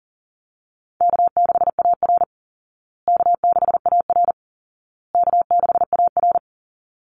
Morse code, heard as a keyed tone, about 40 wpm, 700 Hz. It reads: K6AR K6AR K6AR
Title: Call Signs Repeated 2x - 40wpm